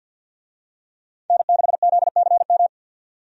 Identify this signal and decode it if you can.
Morse code, heard as a keyed tone, about 46 wpm, 700 Hz. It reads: N6ZCM
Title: USA Call Signs - 5 Characters - 50wpm